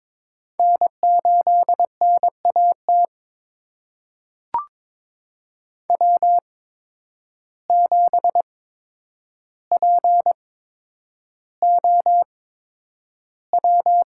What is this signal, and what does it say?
Morse code, heard as a keyed tone, about 22 wpm, spaced as for 9 wpm, 700 Hz. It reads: N8NAT W7POW
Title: USA Call Signs - 8x Character Spacing - 22wpm